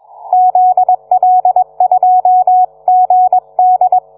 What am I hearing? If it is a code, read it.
Morse code, heard as a keyed tone, about 21 wpm, 740 Hz. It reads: ZL2GD